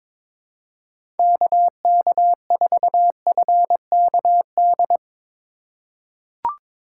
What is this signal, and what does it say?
Morse code, heard as a keyed tone, about 23 wpm, 700 Hz. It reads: KK4FKD
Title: USA Call Signs - 4x Character Spacing - 22wpm